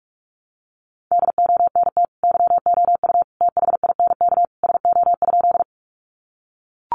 Morse code, heard as a keyed tone, about 44 wpm, 700 Hz. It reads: DONT YOU THINK SO?
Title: Sentences from Top 300 Words 45wpm